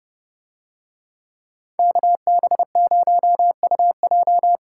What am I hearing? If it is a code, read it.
Morse code, heard as a keyed tone, about 31 wpm, 700 Hz. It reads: KB0UJ